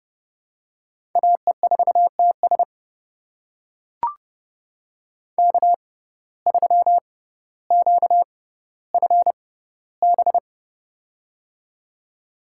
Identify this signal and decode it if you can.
Morse code, heard as a keyed tone, about 30 wpm, spaced as for 15 wpm, 700 Hz. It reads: AE4TS K3QFB
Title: USA Call Signs - 5 Characters - 6x Character Spacing - 30wpm